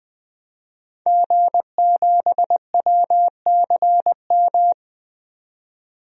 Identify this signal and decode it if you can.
Morse code, heard as a keyed tone, about 20 wpm, 700 Hz. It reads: G7WCM
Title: DXCC-100 Call Signs - 5 Characters - 20wpm